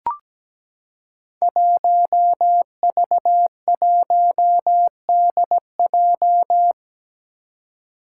Morse code, heard as a keyed tone, about 17 wpm, 700 Hz. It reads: 1V1DJ
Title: Five Challenging Letter-Number Combinations 17wpm